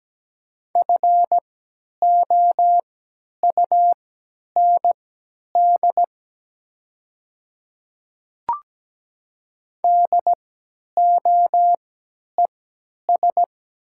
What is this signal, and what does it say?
Morse code, heard as a keyed tone, about 17 wpm, spaced as for 10 wpm, 700 Hz. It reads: FOUND DOES
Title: 3x Character Spacing - Top 300 Words - 17wpm